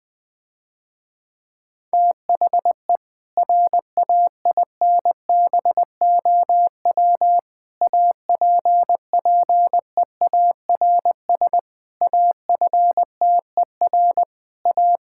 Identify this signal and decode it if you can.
Morse code, heard as a keyed tone, about 20 wpm, 700 Hz. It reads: THE RAINBOW APPEARS AFTER A
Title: Difficult Sentences with Hints - 20wpm